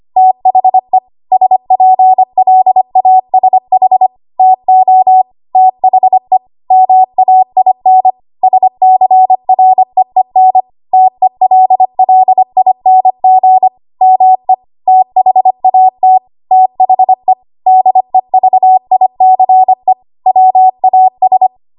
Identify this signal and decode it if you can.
Morse code, heard as a keyed tone, about 25 wpm, 750 Hz. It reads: THE SPLASH TO THE MAIN SCREEN TELLING ME THAT THE DEVICE WAS
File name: Archive / 25 WPM / 25WPM